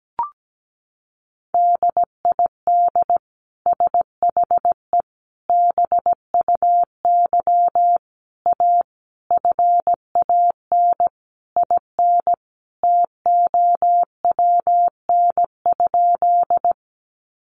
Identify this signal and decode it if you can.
Morse code, heard as a keyed tone, about 17 wpm, 700 Hz. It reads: DID SHE BUY A FAN IN TOWN?
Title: Sentences from Top 1300 Words 17wpm